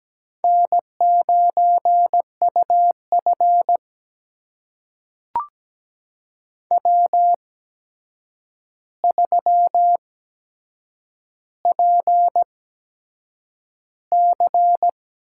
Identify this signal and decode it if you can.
Morse code, heard as a keyed tone, about 17 wpm, spaced as for 7 wpm, 700 Hz. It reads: N9UF W3PC
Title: USA Call Signs - 8x Character Spacing - 17wpm